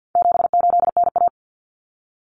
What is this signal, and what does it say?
Morse code, heard as a keyed tone, about 48 wpm, 700 Hz. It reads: 78NA
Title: Common Cut Numbers - 4-Digit 50wpm